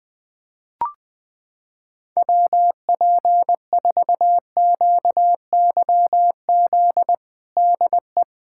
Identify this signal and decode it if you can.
Morse code, heard as a keyed tone, about 21 wpm, 700 Hz. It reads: WP4QYZ DE
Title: USA Callsign DE Callsign - 6 Characters - 20wpm